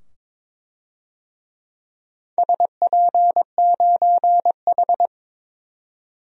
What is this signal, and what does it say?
Morse code, heard as a keyed tone, about 22 wpm, 700 Hz. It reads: SP9H
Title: DXCC-100 Call Signs - 4 Characters - 22wpm